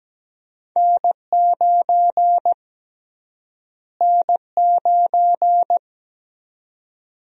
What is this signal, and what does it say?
Morse code, heard as a keyed tone, about 17 wpm, 700 Hz. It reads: N9 N9
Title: USA Call Sign Prefixes - Repeated 2x - 17wpm